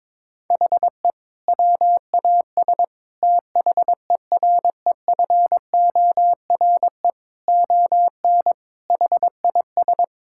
Morse code, heard as a keyed tone, about 22 wpm, 700 Hz. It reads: HE WAS THEREFORE ON HIS
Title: Sentences from Top 1000 Words - Review Part 5 - 22wpm